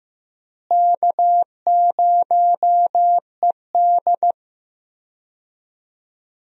Morse code, heard as a keyed tone, about 15 wpm, 700 Hz. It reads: K0ED
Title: USA Call Signs - 4 Characters - Repeated 3x - 15wpm